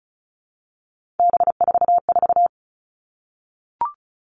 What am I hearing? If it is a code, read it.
Morse code, heard as a keyed tone, about 34 wpm, 700 Hz. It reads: B44
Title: All Cut Numbers - 3-Digit 35wpm